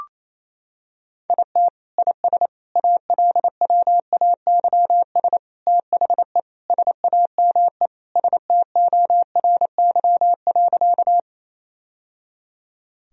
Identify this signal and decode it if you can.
Morse code, heard as a keyed tone, about 28 wpm, 700 Hz. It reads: IT IS ALWAYS THE SAME STORY.
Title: Sentences from Top 1000 Words - Review Part 1 - 28wpm